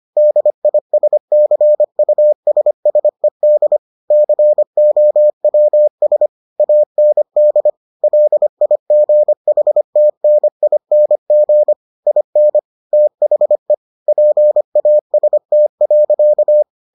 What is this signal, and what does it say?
Morse code, heard as a keyed tone, about 25 wpm, 600 Hz. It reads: DISCUSSED COWS AND LIGHTNING IN THE PAST.